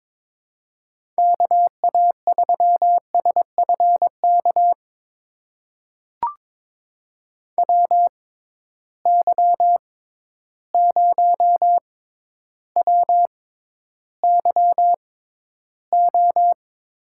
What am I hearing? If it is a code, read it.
Morse code, heard as a keyed tone, about 22 wpm, spaced as for 11 wpm, 700 Hz. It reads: KA3SFK WY0WYO
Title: USA Call Signs - 6 Characters - 6x Character Spacing - 22wpm